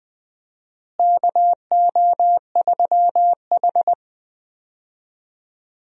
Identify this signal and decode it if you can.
Morse code, heard as a keyed tone, about 20 wpm, 700 Hz. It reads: KO3H
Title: USA Call Signs - 4 Characters - Repeated 3x - 20wpm